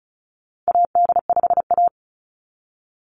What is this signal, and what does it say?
Morse code, heard as a keyed tone, about 34 wpm, 700 Hz. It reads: AD5A
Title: NA SOTA Chaser Callsigns 35wpm